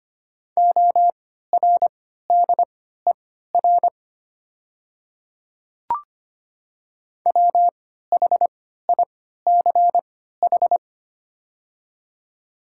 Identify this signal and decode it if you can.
Morse code, heard as a keyed tone, about 25 wpm, spaced as for 14 wpm, 690 Hz. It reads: ORDER WHICH
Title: 3x Character Spacing - Top 500 Words - 25wpm